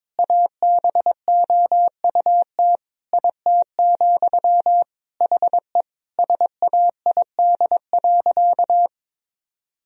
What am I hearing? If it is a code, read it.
Morse code, heard as a keyed tone, about 22 wpm, 700 Hz. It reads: ABOUT IT, HE SAID.
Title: Sentences from Top 1000 Words - Review Part 3 - 22wpm